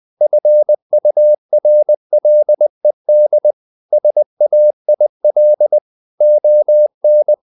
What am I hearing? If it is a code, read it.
Morse code, heard as a keyed tone, about 20 wpm, 600 Hz. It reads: FURLED SAIL ON
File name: the old man and the sea / the old man and the sea part 1 20wpm